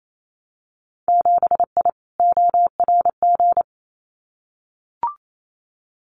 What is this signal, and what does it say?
Morse code, heard as a keyed tone, about 28 wpm, 700 Hz. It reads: 7I ORG
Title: Winter Field Day Exchanges Encore 28wpm